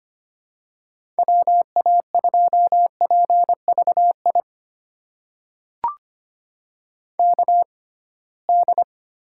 Morse code, heard as a keyed tone, about 25 wpm, spaced as for 19 wpm, 700 Hz. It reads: WA2PVI KD
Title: USA Call Signs - 6 Characters - 6x Character Spacing - 25wpm